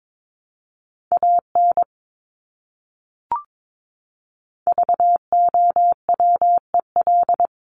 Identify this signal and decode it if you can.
Morse code, heard as a keyed tone, about 22 wpm, 700 Hz. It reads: AN VOWEL
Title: Top 500 Words Repeated 22wpm